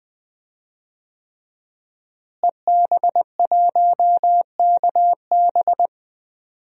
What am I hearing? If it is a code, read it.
Morse code, heard as a keyed tone, about 20 wpm, 700 Hz. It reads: EB1KB
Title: DXCC-100 Call Signs - 5 Characters - Repeated 3x - 20wpm